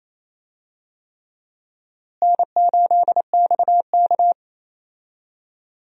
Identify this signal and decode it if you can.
Morse code, heard as a keyed tone, about 28 wpm, 700 Hz. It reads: N8XK
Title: USA Call Signs - 4 Characters - Repeated 3x - 28wpm